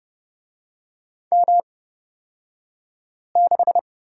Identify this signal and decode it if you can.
Morse code, heard as a keyed tone, about 29 wpm, spaced as for 5 wpm, 700 Hz. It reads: M6
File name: Lesson 36d - BT - Two Characters 30wpm